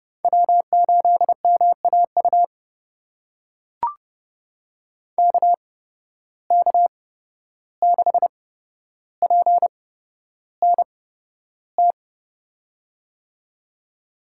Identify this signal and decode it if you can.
Morse code, heard as a keyed tone, about 32 wpm, spaced as for 12 wpm, 700 Hz. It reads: W8MAU KK6PNT
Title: USA Call Signs - 8x Character Spacing - 30wpm